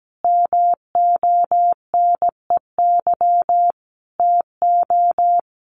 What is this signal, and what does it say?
Morse code, heard as a keyed tone, about 17 wpm, 700 Hz. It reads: MONEY TO
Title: Sets of 5 Words Encore2 17wpm